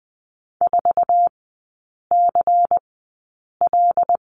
Morse code, heard as a keyed tone, about 20 wpm, spaced as for 8 wpm, 700 Hz. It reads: 4CL
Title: 100 Lesson 17e - Letter u - Three Characters 20wpm